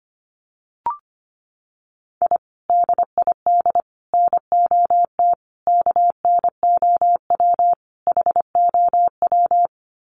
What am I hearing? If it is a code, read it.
Morse code, heard as a keyed tone, about 25 wpm, 700 Hz. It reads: I DID NOT KNOW HOW
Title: Sets of 5 Words Encore 25wpm